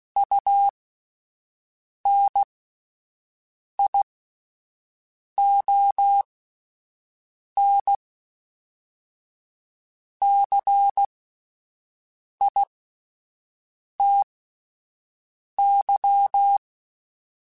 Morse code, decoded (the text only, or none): UNION CITY